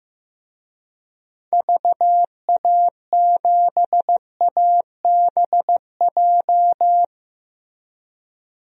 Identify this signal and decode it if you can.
Morse code, heard as a keyed tone, about 15 wpm, 700 Hz. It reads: VA7ABJ
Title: DXCC-100 Call Signs - 6 Characters - 15wpm